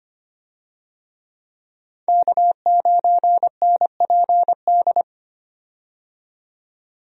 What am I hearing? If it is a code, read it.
Morse code, heard as a keyed tone, about 25 wpm, 690 Hz. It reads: K9NPD
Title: USA Call Signs - 5 Characters - 25wpm